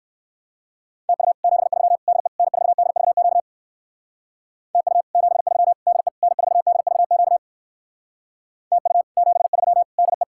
Morse code, heard as a keyed tone, about 58 wpm, 700 Hz. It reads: TU 73 DE N4DVQ TU 73 DE N4DVQ TU 73 DE